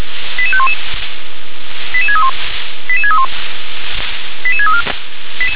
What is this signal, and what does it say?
Five-tone selective call (ZVEI), heard as quick runs of five tones, tones of about 70 ms. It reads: 88511 88631 88631 88634